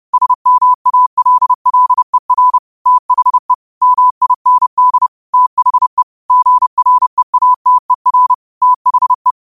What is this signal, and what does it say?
Morse code, heard as a keyed tone, about 30 wpm, 1000 Hz. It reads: SMALLER THE MIND THE GREATER THE
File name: qotdfiles / 30wpm